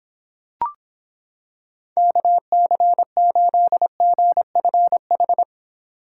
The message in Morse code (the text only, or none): KC8GFH